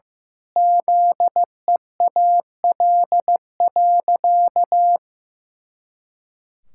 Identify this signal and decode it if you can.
Morse code, heard as a keyed tone, about 15 wpm, 700 Hz. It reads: ZEAL.
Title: Pangram Challenge v2 - 15wpm